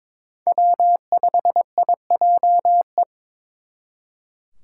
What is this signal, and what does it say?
Morse code, heard as a keyed tone, about 27 wpm, 700 Hz. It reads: W5IJE
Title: Speed Racing - USA 1x3 Call Signs 22wpm